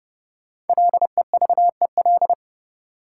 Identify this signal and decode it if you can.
Morse code, heard as a keyed tone, about 30 wpm, 700 Hz. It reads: LEVEL
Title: Sentences from Top 700 Words - New Word Drill - 30wpm